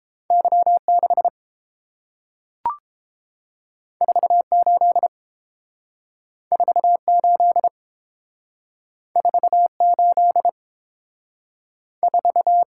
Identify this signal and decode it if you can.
Morse code, heard as a keyed tone, about 30 wpm, 700 Hz. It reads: Y6 48 48 48 4